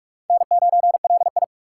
Morse code, heard as a keyed tone, about 41 wpm, 700 Hz. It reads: N9LI